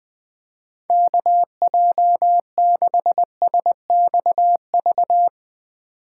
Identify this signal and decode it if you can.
Morse code, heard as a keyed tone, about 20 wpm, 700 Hz. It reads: KJ6SXV
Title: Speed Racing - USA Call Signs 17wpm